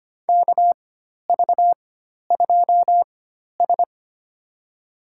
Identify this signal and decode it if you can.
Morse code, heard as a keyed tone, about 25 wpm, spaced as for 12 wpm, 700 Hz. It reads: KV2S
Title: USA Call Signs - 4 Characters - 4x Character Spacing - 25wpm